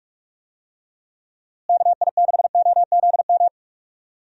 Morse code, heard as a keyed tone, about 46 wpm, 700 Hz. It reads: KI6OZM
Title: USA Call Signs - 6 Characters - Repeated 3x - 45wpm